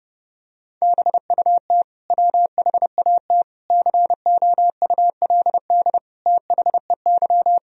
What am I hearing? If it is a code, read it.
Morse code, heard as a keyed tone, about 32 wpm, 700 Hz. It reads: BUT WHAT COULD THEY
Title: Sentences from Top 1000 Words - Review Part 3 - 30wpm